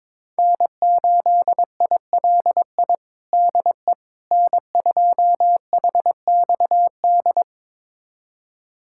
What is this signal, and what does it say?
Morse code, heard as a keyed tone, about 22 wpm, 700 Hz. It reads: N8ILI DE N2HXD
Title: USA Callsign DE Callsign - 5 Characters - 22wpm